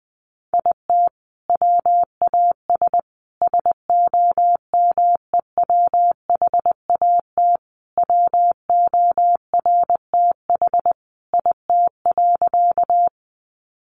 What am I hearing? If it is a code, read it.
Morse code, heard as a keyed tone, about 20 wpm, 700 Hz. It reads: IT WAS SOMEWHAT WORTH IT.
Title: Sentences from Top 1300 Words 20wpm